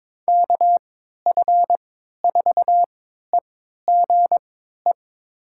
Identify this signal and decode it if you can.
Morse code, heard as a keyed tone, about 22 wpm, spaced as for 12 wpm, 700 Hz. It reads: KF4EGE
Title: USA Call Signs - 6 Characters - 3x Character Spacing - 22wpm